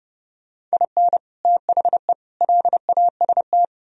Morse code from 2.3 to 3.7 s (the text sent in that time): LAST